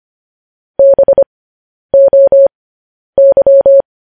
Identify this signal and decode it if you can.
Morse code, heard as a keyed tone, about 25 wpm, spaced as for 10 wpm, 550 Hz. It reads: BOY
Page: news headlines in morse code at 10 wpm